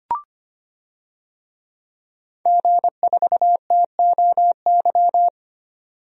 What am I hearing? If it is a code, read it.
Morse code, heard as a keyed tone, about 25 wpm, 700 Hz. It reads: G4TOY
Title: DXCC-100 Call Signs - 5 Characters - 25wpm